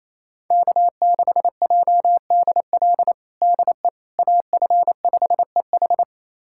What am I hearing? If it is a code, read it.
Morse code, heard as a keyed tone, about 28 wpm, 700 Hz. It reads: K6JDL DE AF5EH
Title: USA Callsign DE Callsign - 5 Characters - 28wpm